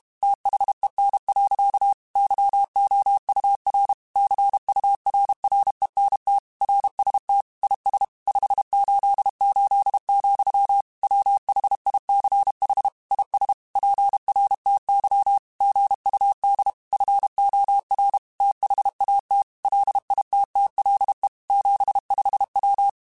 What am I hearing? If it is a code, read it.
Morse code, heard as a keyed tone, about 32 wpm, 800 Hz. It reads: THEN. YOUR CURRENT RST IS 588, WHICH IS PRTY GUD FOR THAT LITTLE 75W
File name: QSOs / QSOs-mp3s / WA7QOX 1@30